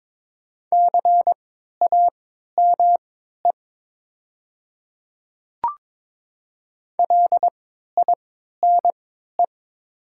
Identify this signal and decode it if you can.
Morse code, heard as a keyed tone, about 22 wpm, spaced as for 12 wpm, 700 Hz. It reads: CAME LINE
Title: 3x Character Spacing - Top 300 Words - 22wpm